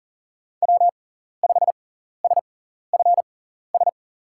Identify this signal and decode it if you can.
Morse code, heard as a keyed tone, about 40 wpm, spaced as for 14 wpm, 700 Hz. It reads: W5SFS
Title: USA Call Signs - 6x Character Spacing - 40wpm